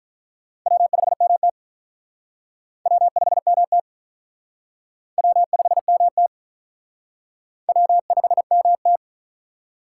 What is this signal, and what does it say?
Morse code, heard as a keyed tone, about 43 wpm, 700 Hz. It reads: W5MT W5MT W5MT W5MT
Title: Speed Racing - USA Call Signs 35wpm